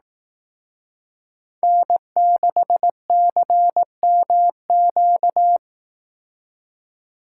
Morse code, heard as a keyed tone, about 18 wpm, 700 Hz. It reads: N6CMQ
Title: Speed Racing - USA 1x3 Call Signs 15wpm